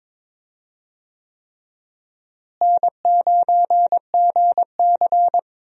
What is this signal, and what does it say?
Morse code, heard as a keyed tone, about 22 wpm, 700 Hz. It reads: N9GC
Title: USA Call Signs - 4 Characters - Repeated 3x - 22wpm